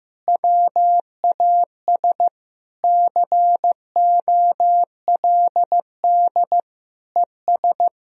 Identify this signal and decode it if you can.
Morse code, heard as a keyed tone, about 15 wpm, 700 Hz. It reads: WAS COLD ES
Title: Sentences from Top 400 Words and Abbreviations 15wpm